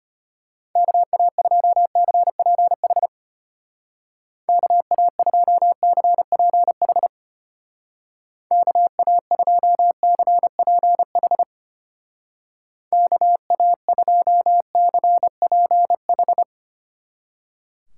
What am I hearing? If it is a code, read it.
Morse code, heard as a keyed tone, about 32 wpm, 700 Hz. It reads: KA2CPH KA2CPH KA2CPH KA2CPH